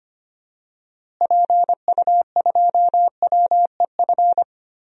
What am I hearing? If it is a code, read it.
Morse code, heard as a keyed tone, about 25 wpm, 700 Hz. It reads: PU2WEF
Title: DXCC-100 Call Signs - 6 Characters - Repeated 3x - 25wpm